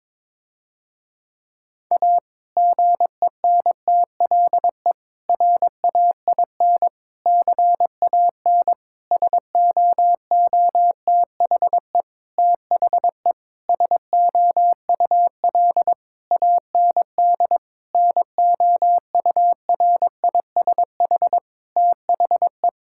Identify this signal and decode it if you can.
Morse code, heard as a keyed tone, about 22 wpm, 700 Hz. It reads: A GENTLE RAIN CAN SOOTHE THE SOUL AND NOURISH THE
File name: Difficult Sentences with Hints - 22wpm